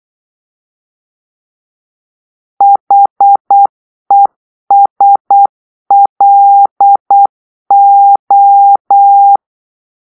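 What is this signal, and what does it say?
Morse code, heard as a keyed tone, about 8 wpm, 800 Hz. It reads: HESLO
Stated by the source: Morse-code cipher made by a player team for the Kachna puzzle game.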